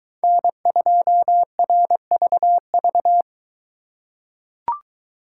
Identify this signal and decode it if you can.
Morse code, heard as a keyed tone, about 22 wpm, 700 Hz. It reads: N2RVV